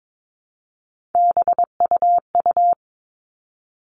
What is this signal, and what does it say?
Morse code, heard as a keyed tone, about 22 wpm, 700 Hz. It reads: BUU
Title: All Cut Numbers - 3-Digit 22wpm